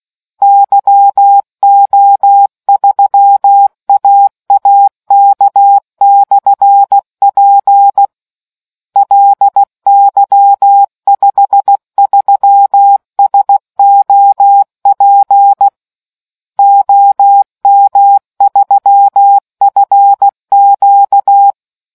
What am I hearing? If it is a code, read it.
Morse code, heard as a keyed tone, about 16 wpm, 800 Hz. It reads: YO3AAK/P LY53SOP OM3FQ